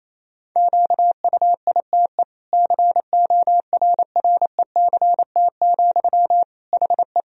Morse code, heard as a keyed tone, about 28 wpm, 700 Hz. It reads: QUITE CORRECT, HE
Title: Sentences from Top 1000 Words - Review Part 2 - 28wpm